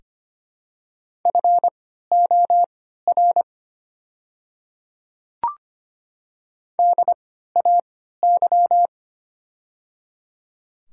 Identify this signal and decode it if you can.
Morse code, heard as a keyed tone, about 25 wpm, spaced as for 14 wpm, 700 Hz. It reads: FOR DAY
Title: 3x Character Spacing - Top 200 Words - 25wpm